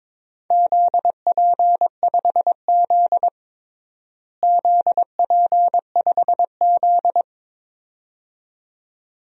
Morse code, heard as a keyed tone, about 22 wpm, 700 Hz. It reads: ZP5Z ZP5Z
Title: DXCC-100 Call Signs - 4 Characters - Repeated 2x - 22wpm